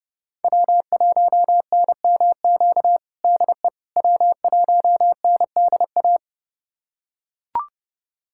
Morse code, heard as a keyed tone, about 30 wpm, 700 Hz. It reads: W1NMQ DE W1NDA
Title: USA Callsign DE Callsign 30wpm